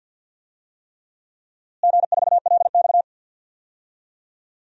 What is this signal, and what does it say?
Morse code, heard as a keyed tone, about 53 wpm, 700 Hz. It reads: G4LX
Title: DXCC-100 Call Signs - 4 Characters - Repeated 3x - 50wpm